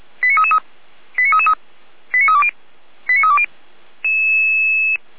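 Five-tone selective call (ZVEI), heard as quick runs of five tones, tones of about 70 ms. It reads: 89333 89333 89320 89320
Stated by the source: Five-tone ZVEI selective call received on an SDR.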